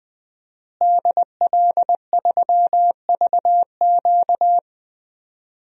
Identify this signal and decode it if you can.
Morse code, heard as a keyed tone, about 20 wpm, 700 Hz. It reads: DL3VQ
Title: DXCC-100 Call Signs - 5 Characters - Repeated 2x - 20wpm